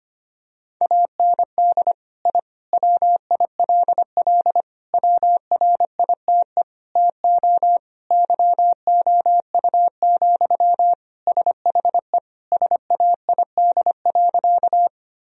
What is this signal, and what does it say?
Morse code, heard as a keyed tone, about 25 wpm, 700 Hz. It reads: AND I WILL WRITE TO YOU, SHE SAID.
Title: Sentences from Top 1000 Words - Review Part 1 - 25wpm